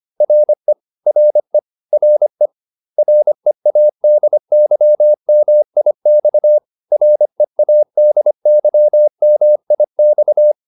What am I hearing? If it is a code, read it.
Morse code, heard as a keyed tone, about 25 wpm, 600 Hz. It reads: RE RE RE READYMIX READYMIX